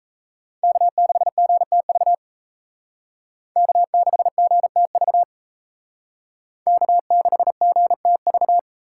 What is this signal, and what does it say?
Morse code, heard as a keyed tone, about 35 wpm, 700 Hz. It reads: K6GTV K6GTV K6GTV
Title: Speed Racing - USA 1x3 Call Signs 28wpm